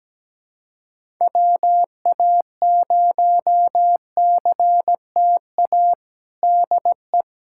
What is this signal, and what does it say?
Morse code, heard as a keyed tone, about 17 wpm, 700 Hz. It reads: WA0CTA DE
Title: USA Callsign DE Callsign - 6 Characters - 17wpm